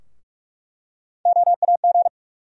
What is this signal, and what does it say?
Morse code, heard as a keyed tone, about 46 wpm, 700 Hz. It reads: OAG
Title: USA Call Sign Postfixes - 3 Letters 45wpm